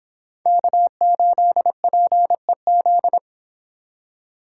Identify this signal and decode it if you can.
Morse code, heard as a keyed tone, about 31 wpm, 700 Hz. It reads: K8PEZ